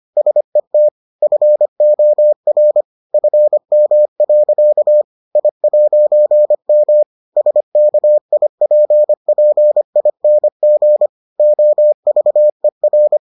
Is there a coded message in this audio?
Yes: Morse code from 0.2 to 13.2 s, heard as a keyed tone, about 25 wpm, 600 Hz. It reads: SET FOR FM. I'M SKIPPING OVER